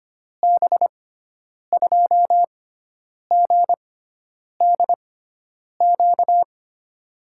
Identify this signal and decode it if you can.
Morse code, heard as a keyed tone, about 25 wpm, spaced as for 9 wpm, 700 Hz. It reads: B2GDQ
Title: USA Call Signs - 6 Characters - 6x Character Spacing - 25wpm